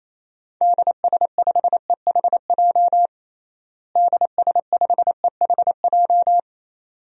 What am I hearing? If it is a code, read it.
Morse code, heard as a keyed tone, about 28 wpm, 700 Hz. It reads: DS5EHJ DS5EHJ
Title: DXCC-100 Call Signs - 6 Characters - Repeated 2x - 28wpm